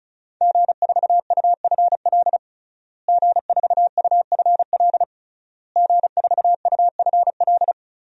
Morse code, heard as a keyed tone, about 34 wpm, 700 Hz. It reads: G4UFL G4UFL G4UFL
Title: DXCC-100 Call Signs - 5 Characters - Repeated 3x - 35wpm